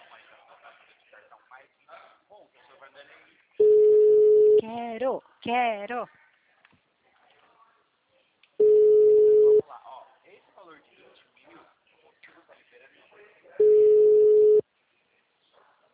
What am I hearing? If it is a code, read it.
Ringback tone (Europe, 425 Hz), with speech.